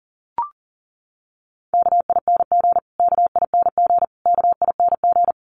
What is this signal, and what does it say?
Morse code, heard as a keyed tone, about 40 wpm, 700 Hz. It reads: KING KING KING
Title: Top 500 Words Repeated 40wpm